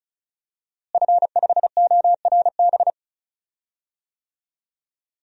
Morse code, heard as a keyed tone, about 36 wpm, 700 Hz. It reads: F5ORB
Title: DXCC-100 Call Signs - 5 Characters - Repeated 2x - 35wpm